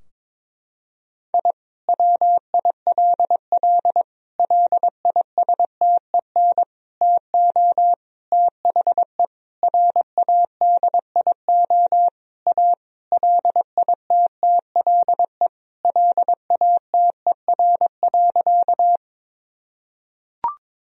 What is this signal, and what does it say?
Morse code, heard as a keyed tone, about 22 wpm, 700 Hz. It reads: I WILL LISTEN TO THE RADIO A LITTLE LATER.
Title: Sentences from Top 1000 Words - Review Part 3 - 22wpm